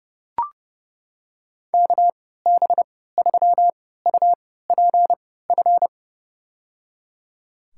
Morse code, heard as a keyed tone, about 30 wpm, spaced as for 17 wpm, 700 Hz. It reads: KB3UPF
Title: USA Call Signs - 6 Characters - 3x Character Spacing - 30wpm